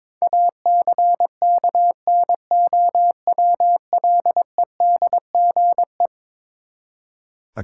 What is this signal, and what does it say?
Morse code, heard as a keyed tone, about 21 wpm, 700 Hz. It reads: ACKNOWLEDGE
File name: practice / 22wpm / Eleven Letter Words - 22wpm